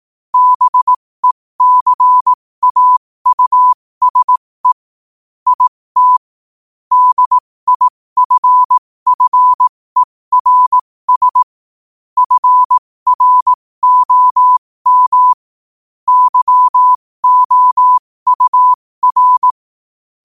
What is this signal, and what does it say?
Morse code, heard as a keyed tone, about 18 wpm, spaced as for 15 wpm, 1000 Hz. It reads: BECAUSE IT DIFFERS FROM YOUR
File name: qotdfiles / 15wpm